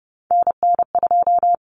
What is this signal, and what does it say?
Morse code, heard as a keyed tone, about 30 wpm, 700 Hz. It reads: NN2
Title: Common Cut Numbers - 3-Digit 30wpm